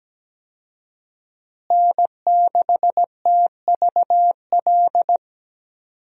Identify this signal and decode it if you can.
Morse code, heard as a keyed tone, about 17 wpm, 700 Hz. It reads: N6TVL
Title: USA Call Signs - 5 Characters - Repeated 2x - 17wpm